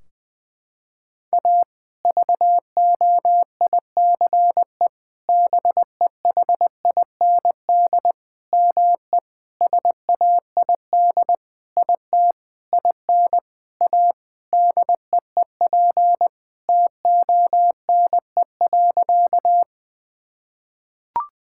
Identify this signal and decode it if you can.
Morse code, heard as a keyed tone, about 20 wpm, 700 Hz. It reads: A VOICE BEHIND ME SAID IT IN A DEEP TONE.
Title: Sentences from Top 1600 Words 20wpm